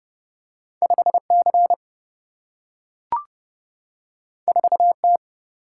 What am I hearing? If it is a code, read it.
Morse code, heard as a keyed tone, about 30 wpm, 700 Hz. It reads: 5C 4T
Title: Single-Letter Single-Number 30wpm